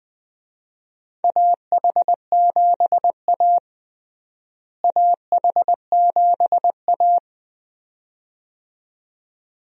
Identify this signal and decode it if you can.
Morse code, heard as a keyed tone, about 20 wpm, 700 Hz. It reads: AH7A AH7A
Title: USA Call Signs - 4 Characters - Repeated 2x - 20wpm